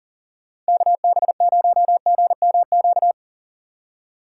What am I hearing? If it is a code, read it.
Morse code, heard as a keyed tone, about 40 wpm, 700 Hz. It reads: KB0GMQ
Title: USA Call Signs - 6 Characters - 2x Character Spacing - 40wpm